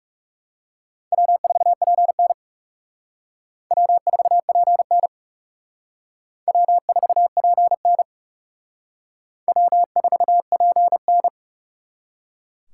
Morse code, heard as a keyed tone, about 37 wpm, 700 Hz. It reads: W4PN W4PN W4PN W4PN
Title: Speed Racing - USA 1x2 Call Signs 30wpm